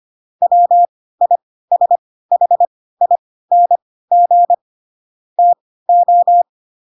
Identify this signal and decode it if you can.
Morse code, heard as a keyed tone, about 25 wpm, spaced as for 16 wpm, 700 Hz. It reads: WISHING TO
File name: Pinocchio / Pinocchio ch3 10 25 ik1hge